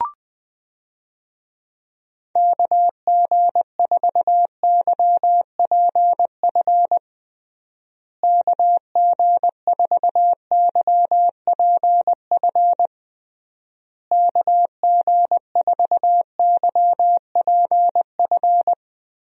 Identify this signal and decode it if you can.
Morse code, heard as a keyed tone, about 20 wpm, 700 Hz. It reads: KG4YPF KG4YPF KG4YPF
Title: USA Call Signs - 6 Characters - Repeated 3x - 20wpm